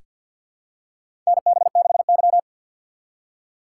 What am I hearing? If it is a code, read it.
Morse code, heard as a keyed tone, about 47 wpm, 700 Hz. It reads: NB6Y
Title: USA Call Signs - 4 Characters - 3x Character Spacing - 50wpm